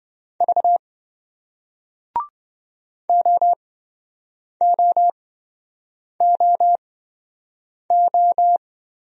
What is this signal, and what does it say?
Morse code, heard as a keyed tone, about 27 wpm, spaced as for 7 wpm, 700 Hz. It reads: V OOOO